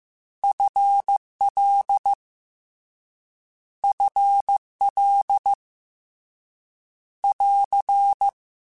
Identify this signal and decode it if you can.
Morse code, heard as a keyed tone, about 15 wpm, 800 Hz. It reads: FL FL <AR>
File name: Short Runs / Short Runs-mp3s / ShortContrasts @13